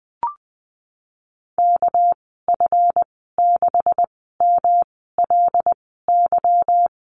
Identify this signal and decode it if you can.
Morse code, heard as a keyed tone, about 20 wpm, spaced as for 15 wpm, 700 Hz. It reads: KF6MLY
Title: 207 Lesson 33g - Letter j - Call Signs 20wpm